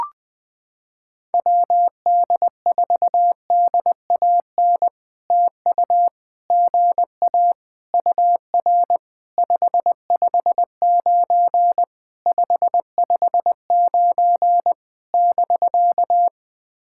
Morse code, heard as a keyed tone, about 21 wpm, 700 Hz. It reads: WD4DAN TU GA UR 559 559 <BK>
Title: POTA QSO Practice - 20wpm